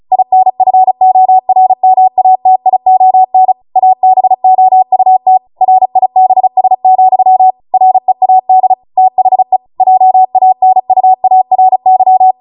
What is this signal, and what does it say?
Morse code, heard as a keyed tone, about 36 wpm, 750 Hz. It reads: INFORMATION ABOUT RIBS, READ THE JANUARY